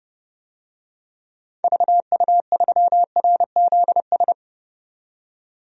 Morse code, heard as a keyed tone, about 30 wpm, 700 Hz. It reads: VU3RZS
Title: DXCC-100 Call Signs - 6 Characters - 30wpm